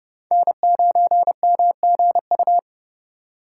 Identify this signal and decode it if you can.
Morse code, heard as a keyed tone, about 28 wpm, 700 Hz. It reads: N9MGU